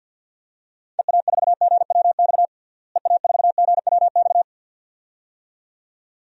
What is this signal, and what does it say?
Morse code, heard as a keyed tone, about 47 wpm, 700 Hz. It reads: EA4GWX EA4GWX
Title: DXCC-100 Call Signs - 6 Characters - Repeated 2x - 50wpm